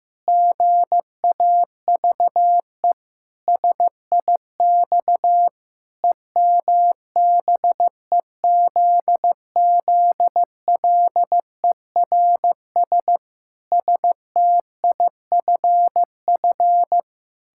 Morse code, heard as a keyed tone, about 15 wpm, 700 Hz. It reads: GAVE SIX EMBEZZLERS STIFF
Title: Pangram Challenge v2 - 15wpm